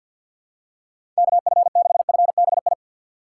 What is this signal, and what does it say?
Morse code, heard as a keyed tone, about 52 wpm, 700 Hz. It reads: KF6FBI
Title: USA Call Signs - 6 Characters - 4x Character Spacing - 50wpm